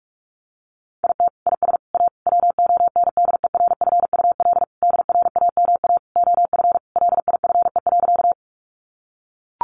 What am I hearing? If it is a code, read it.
Morse code, heard as a keyed tone, about 44 wpm, 700 Hz. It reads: IT IS A WONDERFUL DRAMA OF LIFE.
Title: Sentences from Top 1500 Words 45wpm